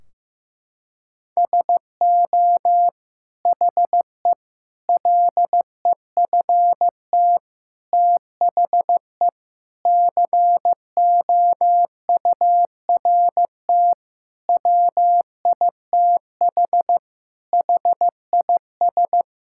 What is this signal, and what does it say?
Morse code, heard as a keyed tone, about 15 wpm, 700 Hz. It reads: SO HE LEFT THE COURT WITH HIS
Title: Sentences from Top 1600 Words 15wpm